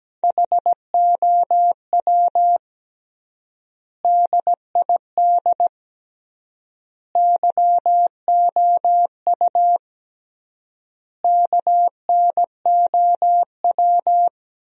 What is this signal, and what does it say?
Morse code, heard as a keyed tone, about 17 wpm, 700 Hz. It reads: HOW DID YOU KNOW
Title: 3x Word-Spacing - Sentences from Top 100 Words - 17wpm